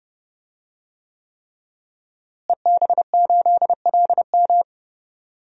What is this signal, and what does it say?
Morse code, heard as a keyed tone, about 30 wpm, 700 Hz. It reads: EB8LM
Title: DXCC-100 Call Signs - 5 Characters - Repeated 3x - 30wpm